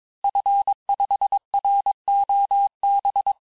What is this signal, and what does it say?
Morse code, heard as a keyed tone, about 22 wpm, 800 Hz. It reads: F5ROB
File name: cours-du-mercredi / audios / cours-854-liste-fin-QSO-vitesse-21-mpm